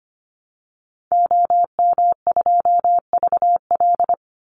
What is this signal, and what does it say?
Morse code, heard as a keyed tone, about 25 wpm, 700 Hz. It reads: OM2VL
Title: Call Signs Repeated 3x - 25wpm